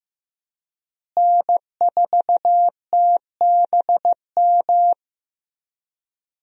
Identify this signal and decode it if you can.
Morse code, heard as a keyed tone, about 15 wpm, 700 Hz. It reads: N4TBM
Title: USA Call Signs - 5 Characters - Repeated 3x - 15wpm